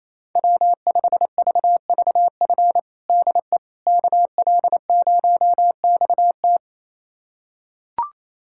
Morse code, heard as a keyed tone, about 28 wpm, 700 Hz. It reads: W5VVF DE KL0XT
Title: USA Callsign DE Callsign - 5 Characters - 28wpm